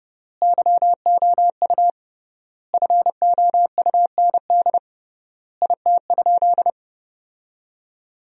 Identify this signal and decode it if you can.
Morse code, heard as a keyed tone, about 30 wpm, 700 Hz. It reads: YOU FOUND IT?
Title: 3x Word-Spacing - Sentences from Top 300 Words - 30wpm